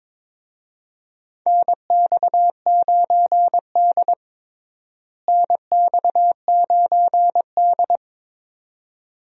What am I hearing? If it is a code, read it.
Morse code, heard as a keyed tone, about 22 wpm, 700 Hz. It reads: NX9D NX9D
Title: USA Call Signs - 4 Characters - Repeated 2x - 22wpm